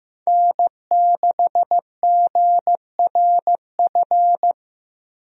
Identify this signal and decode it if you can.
Morse code, heard as a keyed tone, about 16 wpm, 700 Hz. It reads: N6GRF
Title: USA Call Signs - 5 Characters - Repeated 3x - 15wpm